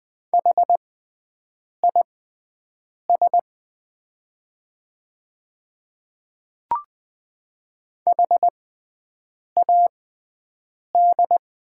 Morse code, heard as a keyed tone, about 20 wpm, spaced as for 7 wpm, 700 Hz. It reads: HIS HAD
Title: 6x Character Spacing - Top 200 Words - 20wpm